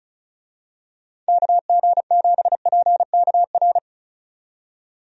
Morse code, heard as a keyed tone, about 36 wpm, 700 Hz. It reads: KG7PKR
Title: USA Call Signs - 6 Characters - 35wpm